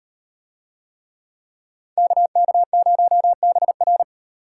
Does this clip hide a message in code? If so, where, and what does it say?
Yes: Morse code from 2.0 to 4.0 s, heard as a keyed tone, about 38 wpm, 700 Hz. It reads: KK0BR